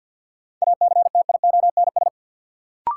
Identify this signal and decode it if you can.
Morse code, heard as a keyed tone, about 47 wpm, 700 Hz. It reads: ACTIONS
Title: Sentences from Top 1000 Words - Review Part 6 - New Word Drill - 50wpm